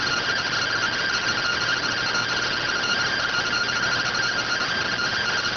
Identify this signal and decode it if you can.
Two-tone teleprinter signal, about 170 Hz shift.